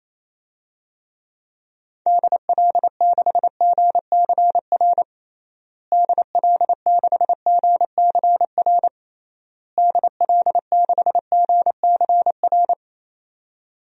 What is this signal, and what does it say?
Morse code, heard as a keyed tone, about 27 wpm, 700 Hz. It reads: DL6GCR DL6GCR DL6GCR
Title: DXCC-100 Call Signs - 6 Characters - Repeated 3x - 28wpm